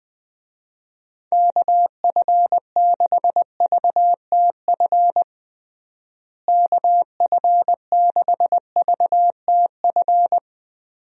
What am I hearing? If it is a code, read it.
Morse code, heard as a keyed tone, about 20 wpm, 700 Hz. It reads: KF6VTF KF6VTF
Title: USA Call Signs - 6 Characters - Repeated 3x - 20wpm